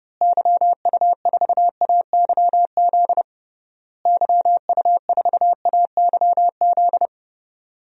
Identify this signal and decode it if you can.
Morse code, heard as a keyed tone, about 30 wpm, 700 Hz. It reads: YU4AYZ YU4AYZ
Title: DXCC-100 Call Signs - 6 Characters - Repeated 3x - 30wpm